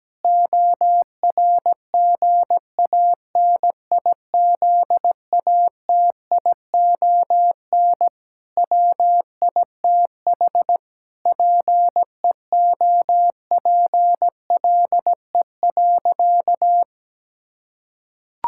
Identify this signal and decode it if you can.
Morse code, heard as a keyed tone, about 17 wpm, 700 Hz. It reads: ORGANIZATION WITH PEOPLE.
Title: Sentences from Top 1000 Words - Review Part 5 - 17wpm